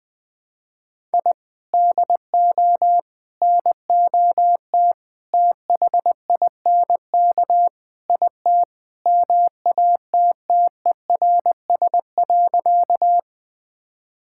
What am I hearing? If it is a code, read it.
Morse code, heard as a keyed tone, about 19 wpm, 700 Hz. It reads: I DO NOT THINK IT MATTERS.
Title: Sentences with 5-Word Phrases - Spoken Beginning - 20wpm